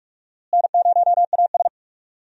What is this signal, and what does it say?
Morse code, heard as a keyed tone, about 42 wpm, 700 Hz. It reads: N0AS
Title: USA Call Signs - 4 Characters - Repeated 2x - 45wpm